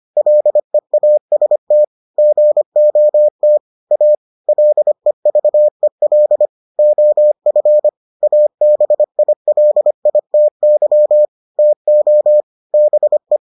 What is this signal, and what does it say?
Morse code, heard as a keyed tone, about 25 wpm, 600 Hz. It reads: LEAST GOT A LEVEL OF ABILITY TO BE